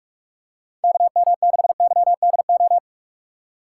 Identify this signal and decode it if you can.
Morse code, heard as a keyed tone, about 41 wpm, 700 Hz. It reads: KM6YDO
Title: USA Call Signs - 6 Characters - 2x Character Spacing - 45wpm